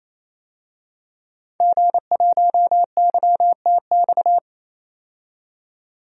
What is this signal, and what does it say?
Morse code, heard as a keyed tone, about 28 wpm, 700 Hz. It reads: G1YTX
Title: DXCC-100 Call Signs - 5 Characters - 28wpm